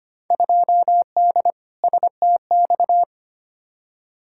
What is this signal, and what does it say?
Morse code, heard as a keyed tone, about 25 wpm, 700 Hz. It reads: 2D STX
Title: Field Day Exchanges - 25wpm